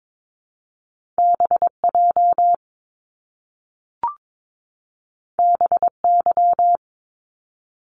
Morse code, heard as a keyed tone, about 22 wpm, 700 Hz. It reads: BJ BY
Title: Top Two Letter Combinations in Words 22wpm